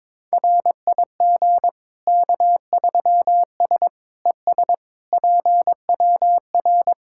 Morse code, heard as a keyed tone, about 22 wpm, 700 Hz. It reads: RIG K3S ES PWR
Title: Practice QSO - Exchange 2 - Info - 22wpm